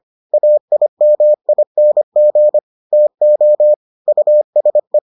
Morse code, heard as a keyed tone, about 25 wpm, 600 Hz. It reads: AIMING TO USE